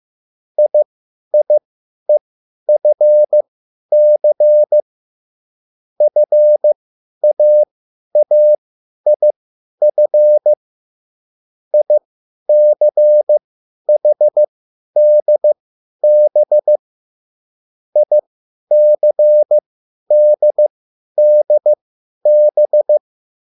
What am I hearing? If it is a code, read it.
Morse code, heard as a keyed tone, about 15 wpm, spaced as for 10 wpm, 600 Hz. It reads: IIEFC FAAIF ICHDB ICDDB